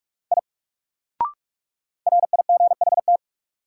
Morse code, heard as a keyed tone, about 42 wpm, 700 Hz. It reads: I RIGHT
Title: Sentences from Top 200 Words - New Word Drill - 45wpm